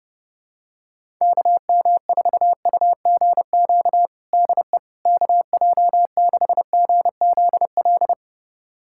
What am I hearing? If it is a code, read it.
Morse code, heard as a keyed tone, about 32 wpm, 700 Hz. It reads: KM4UGQ DE KJ6GZL
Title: USA Callsign DE Callsign 30wpm